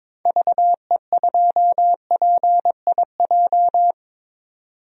Morse code, heard as a keyed tone, about 21 wpm, 700 Hz. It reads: VE2PIJ
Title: DXCC-100 Call Signs - 6 Characters - 22wpm